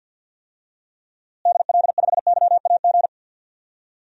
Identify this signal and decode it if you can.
Morse code, heard as a keyed tone, about 46 wpm, 700 Hz. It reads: DL5YAG
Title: DXCC-100 Call Signs - 6 Characters - Repeated 3x - 50wpm